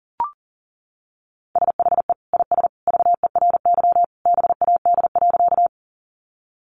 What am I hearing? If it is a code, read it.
Morse code, heard as a keyed tone, about 37 wpm, 700 Hz. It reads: SHE IS VERY BAD.
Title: Sentences from Top 700 Words 40wpm